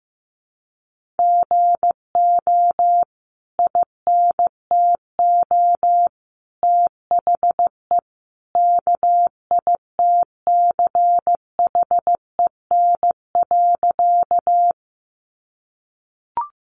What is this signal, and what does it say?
Morse code, heard as a keyed tone, about 15 wpm, 700 Hz. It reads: GO INTO THE KITCHEN.
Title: Sentences from Top 1400 Words 15wpm